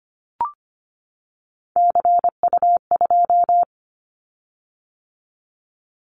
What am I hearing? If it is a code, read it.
Morse code, heard as a keyed tone, about 25 wpm, 700 Hz. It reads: CU2
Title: Callsign Prefixes for Top 50 DXCC - 25wpm